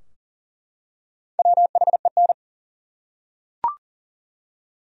Morse code, heard as a keyed tone, about 40 wpm, 700 Hz. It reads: WHEN
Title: Top 100 Words 40wpm